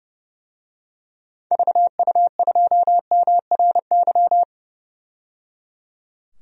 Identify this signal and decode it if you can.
Morse code, heard as a keyed tone, about 30 wpm, 700 Hz. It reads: VU2MRY
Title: DXCC-100 Call Signs - 6 Characters - 30wpm